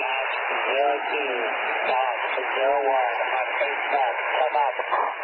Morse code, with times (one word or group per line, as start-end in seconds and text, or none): none